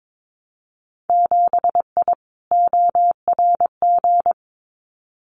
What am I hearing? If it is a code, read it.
Morse code, heard as a keyed tone, about 22 wpm, 700 Hz. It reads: 7I ORG
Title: Winter Field Day Exchanges Encore 22wpm